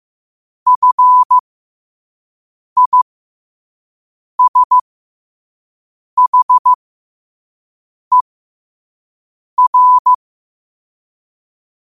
Morse code, heard as a keyed tone, about 15 wpm, spaced as for 5 wpm, 1000 Hz. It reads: FISHER
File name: qotdfiles / 5wpm